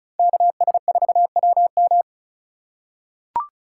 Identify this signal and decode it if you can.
Morse code, heard as a keyed tone, about 36 wpm, 700 Hz. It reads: KS4WM
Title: USA Call Signs - 5 Characters - 4x Character Spacing - 35wpm